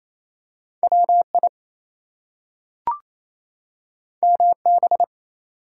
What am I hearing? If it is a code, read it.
Morse code, heard as a keyed tone, about 28 wpm, 700 Hz. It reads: WI MB
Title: US States and Canadian Provinces 28wpm